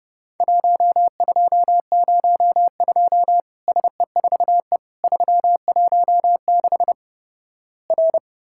Morse code, heard as a keyed tone, about 31 wpm, 700 Hz. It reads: 1202 SE4E 316 R
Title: Worked All Europe DX Contest - QTC Exchange - 30wpm